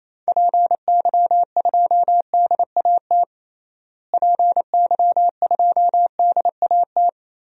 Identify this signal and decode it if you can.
Morse code, heard as a keyed tone, about 28 wpm, 700 Hz. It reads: PY2DAT PY2DAT
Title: DXCC-100 Call Signs - 6 Characters - Repeated 3x - 28wpm